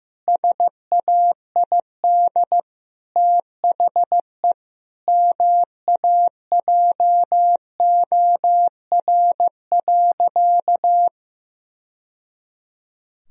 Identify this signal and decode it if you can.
Morse code, heard as a keyed tone, about 16 wpm, 700 Hz. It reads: SAID THE MAJOR.
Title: Sentences from Top 1000 Words - Review Part 4 - 15wpm